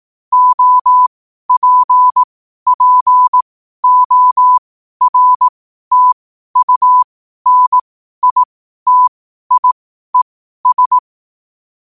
Morse code, heard as a keyed tone, about 18 wpm, spaced as for 12 wpm, 1000 Hz. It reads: OPPORTUNITIES
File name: qotdfiles / 12wpm